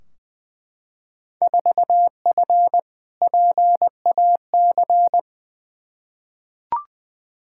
Morse code, heard as a keyed tone, about 20 wpm, 700 Hz. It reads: 4F PAC
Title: Field Day Exchanges Encore - 20wpm